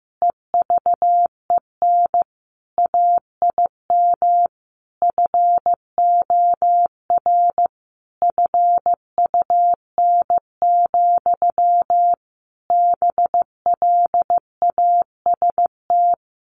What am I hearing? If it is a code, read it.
Morse code, heard as a keyed tone, about 15 wpm, 700 Hz. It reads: EVEN AIM FOR FUN, BLAST
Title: A Sound Of Thunder 15wpm